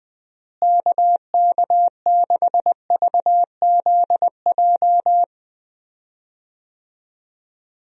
Morse code, heard as a keyed tone, about 20 wpm, 700 Hz. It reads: KK6VZJ
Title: USA Call Signs - 6 Characters - Repeated 2x - 20wpm